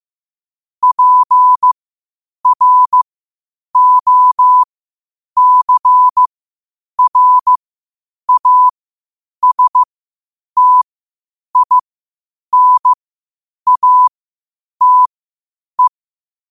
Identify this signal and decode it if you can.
Morse code, heard as a keyed tone, about 15 wpm, spaced as for 8 wpm, 1000 Hz. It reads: PROCRASTINATE